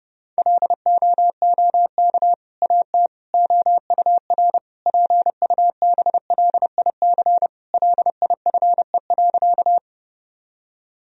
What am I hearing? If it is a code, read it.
Morse code, heard as a keyed tone, about 32 wpm, 700 Hz. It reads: LOOK AT OUR PUBLIC LIFE.
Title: Sentences from Top 1000 Words - Review Part 3 - 30wpm